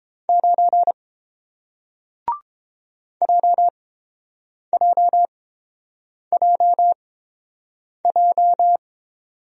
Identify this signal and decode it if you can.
Morse code, heard as a keyed tone, about 30 wpm, spaced as for 7 wpm, 700 Hz. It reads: 9 JJJJ